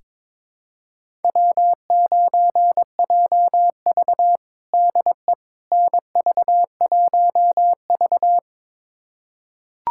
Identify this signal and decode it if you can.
Morse code, heard as a keyed tone, about 22 wpm, 700 Hz. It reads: W9JV DE NV1V
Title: USA Callsign DE Callsign - 4 Characters - 22wpm